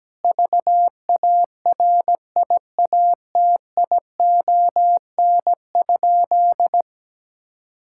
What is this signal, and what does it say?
Morse code, heard as a keyed tone, about 17 wpm, 700 Hz. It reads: VARIATION?
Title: Sentences from Top 1600 Words 17wpm